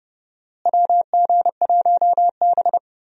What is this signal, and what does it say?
Morse code, heard as a keyed tone, about 30 wpm, 700 Hz. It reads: WG1B
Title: USA Call Signs - 4 Characters - Repeated 2x - 30wpm